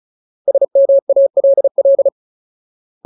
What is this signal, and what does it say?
Morse code, heard as a keyed tone, about 34 wpm, 550 Hz. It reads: SMALL
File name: Ultimate Word List / 35WPM Ultimate